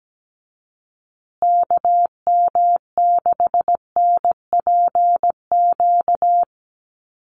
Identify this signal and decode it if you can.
Morse code, heard as a keyed tone, about 17 wpm, 700 Hz. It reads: KM6NPQ
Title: USA Call Signs Repeated 3x 17wpm